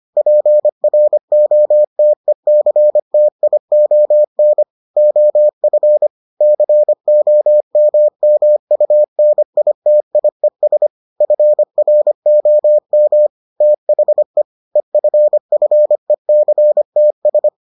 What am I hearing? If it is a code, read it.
Morse code, heard as a keyed tone, about 25 wpm, 600 Hz. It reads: PROTECTION OF COMMUNITIES FROM THE EFFECTS